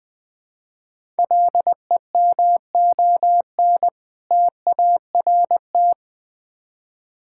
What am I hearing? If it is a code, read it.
Morse code, heard as a keyed tone, about 20 wpm, 700 Hz. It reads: LEMON TART
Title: Difficult Words with Hints - 20wpm